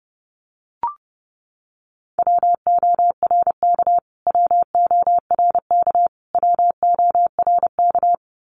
Morse code, heard as a keyed tone, about 30 wpm, 700 Hz. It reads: WORK WORK WORK
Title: Top 500 Words Repeated 30wpm